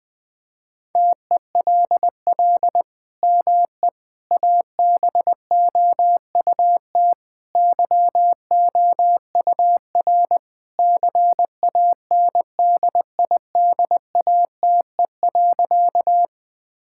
Morse code, heard as a keyed tone, about 21 wpm, 700 Hz. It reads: TELL ME ABOUT YOUR CANDIDATE.
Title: Sentences from Top 1600 Words 20wpm